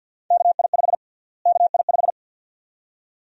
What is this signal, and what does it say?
Morse code, heard as a keyed tone, about 53 wpm, 700 Hz. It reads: KI5 KI5
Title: USA Call Sign Prefixes - Repeated 2x - 50wpm